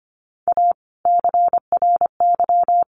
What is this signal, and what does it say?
Morse code, heard as a keyed tone, about 25 wpm, 700 Hz. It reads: A CRY
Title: Sentences from Top 1300 Words 25wpm